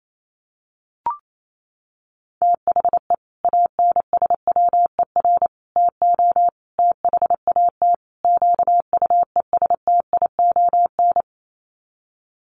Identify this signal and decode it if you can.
Morse code, heard as a keyed tone, about 28 wpm, 700 Hz. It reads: THE ANSWER TO THAT QUESTION
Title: Sets of 5 Words Encore2 28wpm